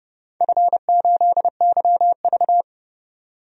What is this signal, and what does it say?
Morse code, heard as a keyed tone, about 30 wpm, 700 Hz. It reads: F8YV
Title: DXCC-100 Call Signs - 4 Characters - 30wpm